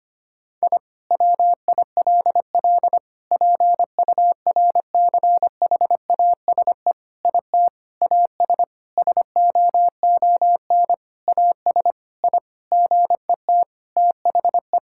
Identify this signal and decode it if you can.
Morse code, heard as a keyed tone, about 25 wpm, 700 Hz. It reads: I WILL PURCHASE IT AS SOON AS I GET THE
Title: Sentences from Top 1000 Words - Review Part 5 - 25wpm